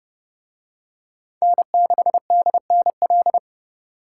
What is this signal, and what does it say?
Morse code, heard as a keyed tone, about 28 wpm, 700 Hz. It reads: N6DNL